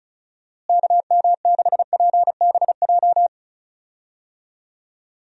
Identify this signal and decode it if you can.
Morse code, heard as a keyed tone, about 36 wpm, 700 Hz. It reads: KM6PBJ
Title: USA Call Signs - 6 Characters - Repeated 2x - 35wpm